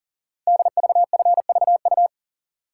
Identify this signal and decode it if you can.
Morse code, heard as a keyed tone, about 40 wpm, 700 Hz. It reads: DVFVU
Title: Code Groups - 5 Letters - 40wpm